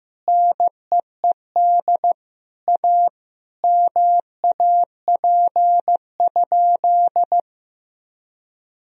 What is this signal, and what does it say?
Morse code, heard as a keyed tone, about 15 wpm, 700 Hz. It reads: NEED A MAP?
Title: Sentences from Top 1000 Words - Review Part 2 - 15wpm